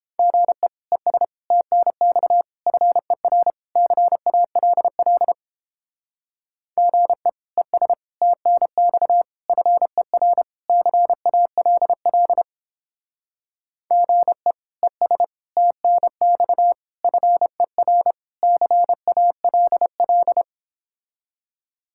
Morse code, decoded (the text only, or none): GE ES TNX FER CALL GE ES TNX FER CALL GE ES TNX FER CALL